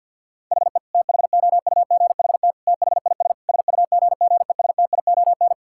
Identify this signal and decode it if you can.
Morse code, heard as a keyed tone, about 50 wpm, 700 Hz. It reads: HE THOUGHT THIS SUGGESTION
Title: Sentences from Top 1600 Words 50wpm